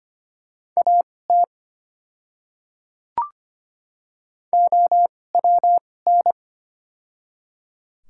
Morse code, heard as a keyed tone, about 25 wpm, spaced as for 18 wpm, 700 Hz. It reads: AT OWN
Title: 2x Character Spacing - Top 300 Words - 25wpm